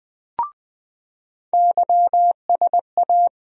Morse code, heard as a keyed tone, about 20 wpm, 700 Hz. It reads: YSA
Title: USA Call Sign Postfixes - 3 Letters 20wpm